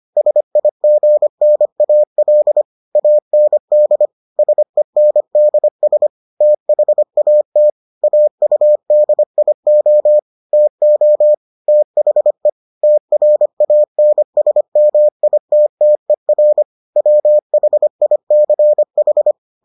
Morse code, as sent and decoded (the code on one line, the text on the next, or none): ... .. --. -. .- .-.. / .- -. -.. / ... -. -.. ... / - .... .- - / .- ..- -.. .. --- / - --- / - .... / - .-. .- -. ... -- .. - - . .-. / .-- .... .. -.-. ....
SIGNAL AND SENDS THAT AUDIO TO THE TRANSMITTER WHICH